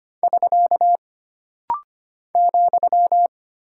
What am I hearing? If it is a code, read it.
Morse code, heard as a keyed tone, about 25 wpm, 700 Hz. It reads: <SK> ,
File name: Single Punctuation-Prosign - Rapid-Fire - 25wpm